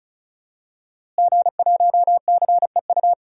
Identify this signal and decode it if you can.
Morse code, heard as a keyed tone, about 36 wpm, 700 Hz. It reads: G1CEU
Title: DXCC-100 Call Signs - 5 Characters - Repeated 2x - 35wpm